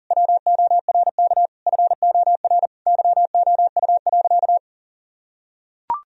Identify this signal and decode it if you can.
Morse code, heard as a keyed tone, about 40 wpm, 700 Hz. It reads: WORK FOR YOU.
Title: Sentences from Top 1000 Words - Review Part 2 - 40wpm